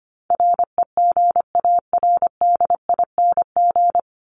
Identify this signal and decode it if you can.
Morse code, heard as a keyed tone, about 25 wpm, 700 Hz. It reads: REGARDING
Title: ING Suffix - Top 100 Words with Base Words 25wpm